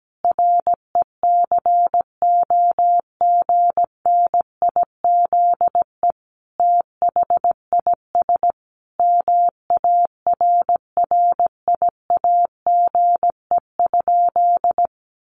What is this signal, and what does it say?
Morse code, heard as a keyed tone, about 17 wpm, 700 Hz. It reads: RECOGNIZE THIS MARRIAGE?
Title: Sentences from Top 1400 Words 17wpm